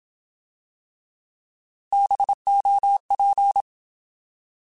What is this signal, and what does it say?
Morse code, heard as a keyed tone, about 27 wpm, 800 Hz. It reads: BOP
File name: Alphabetical / Alphabetical mp3s / Bs @25